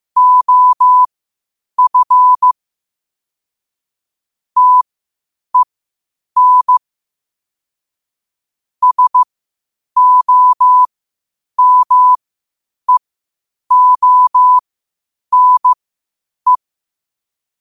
Morse code, heard as a keyed tone, about 15 wpm, spaced as for 8 wpm, 1000 Hz. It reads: OF TEN SOMEONE
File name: qotdfiles / 8wpm